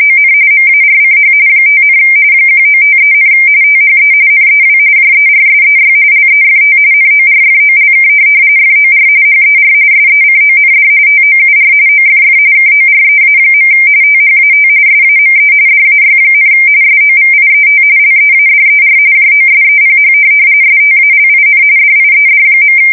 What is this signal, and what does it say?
RTTY, about 45.45 baud. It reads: ON MARCH 5, 1953, THE NEW JERSEY LEFT NORFOLK FOR HER SECOND TOUR OFF KOREA. THAT SAME DAY, SOVIET LEADER JOSEPH STALIN DIED.
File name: fsk170 / USS-New-Jersey